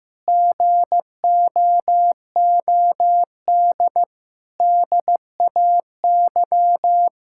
Morse code, heard as a keyed tone, about 15 wpm, 700 Hz. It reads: GOOD DAY
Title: Sentences from Top 1000 Words - Review Part 6 - 15wpm